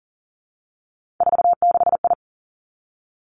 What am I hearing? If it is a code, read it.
Morse code, heard as a keyed tone, about 41 wpm, 700 Hz. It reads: 46I
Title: Three Challenging Letter-Number Combinations 40wpm